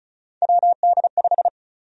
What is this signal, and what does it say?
Morse code, heard as a keyed tone, about 36 wpm, 700 Hz. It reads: WD5
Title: USA Call Sign Prefixes - 35wpm